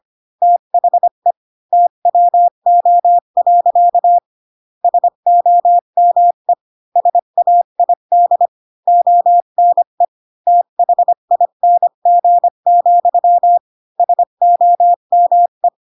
Morse code, heard as a keyed tone, about 25 wpm, 700 Hz. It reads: THE TWO. SOME SAID ONE THING, SOME
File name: Pinocchio / Pinocchio ch3 20 25 ik1hge